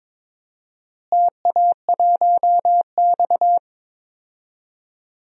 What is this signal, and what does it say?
Morse code, heard as a keyed tone, about 22 wpm, 700 Hz. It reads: TA1X
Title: DXCC-100 Call Signs - 4 Characters - Repeated 3x - 22wpm